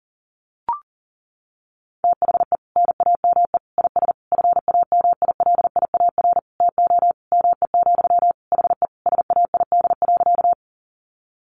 Morse code, heard as a keyed tone, about 40 wpm, 700 Hz. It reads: THE NAME IS FAMILIAR TO ME, HE SAID.
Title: Sentences from Top 1000 Words 40wpm